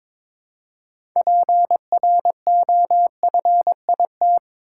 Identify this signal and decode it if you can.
Morse code, heard as a keyed tone, about 22 wpm, 700 Hz. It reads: PROFIT
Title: Six Letter Words - 22wpm